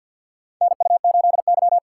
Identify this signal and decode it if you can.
Morse code, heard as a keyed tone, about 52 wpm, 690 Hz. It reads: NU8Y
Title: USA Call Signs - 3x Character Spacing - 50wpm